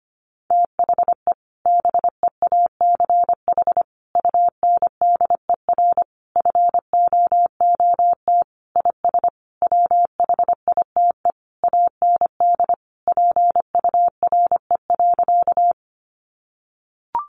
Morse code, heard as a keyed tone, about 25 wpm, 700 Hz. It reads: THE BEACH UNDER FOOT IS WHITE AND PURE.
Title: Sentences from Top 1500 Words 25wpm